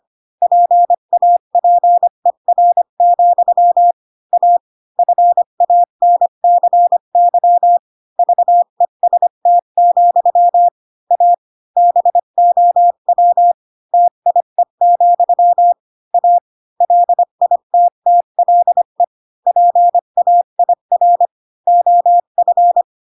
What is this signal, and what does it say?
Morse code, heard as a keyed tone, about 25 wpm, 700 Hz. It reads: PAPER, A FANCY VEST, A BOW TIE, A LITTLE PAIR OF
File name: Pinocchio / Pinocchio ch8 20 25 ik1hge